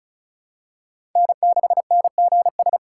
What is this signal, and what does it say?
Morse code, heard as a keyed tone, about 36 wpm, 700 Hz. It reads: N6NGS
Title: USA Call Signs - 5 Characters - 3x Character Spacing - 35wpm